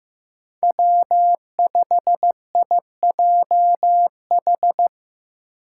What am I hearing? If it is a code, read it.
Morse code, heard as a keyed tone, about 15 wpm, 700 Hz. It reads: W5IJH
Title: K1USN SST QSO Practice - 15wpm